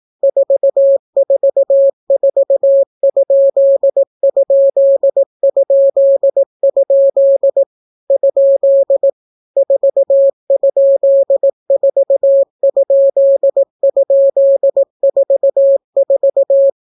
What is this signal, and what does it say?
Morse code, heard as a keyed tone, about 18 wpm, 550 Hz. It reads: 444???? ? 4?4??44